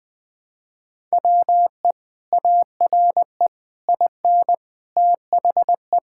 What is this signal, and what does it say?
Morse code, heard as a keyed tone, about 20 wpm, 700 Hz. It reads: WE ARE IN THE
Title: Sentences with 7-Word Phrases - 20wpm